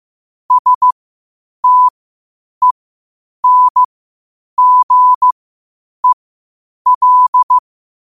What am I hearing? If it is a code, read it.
Morse code, heard as a keyed tone, about 15 wpm, spaced as for 8 wpm, 1000 Hz. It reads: STENGEL
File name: qotdfiles / 8wpm